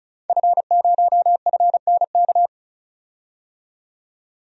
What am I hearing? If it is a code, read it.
Morse code, heard as a keyed tone, about 36 wpm, 700 Hz. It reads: F0FNK
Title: DXCC-100 Call Signs - 5 Characters - 35wpm